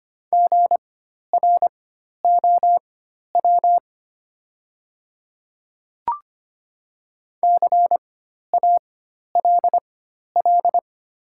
Morse code, heard as a keyed tone, about 25 wpm, spaced as for 12 wpm, 690 Hz. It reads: GROW CALL
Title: 4x Character Spacing - Top 300 Words - 25wpm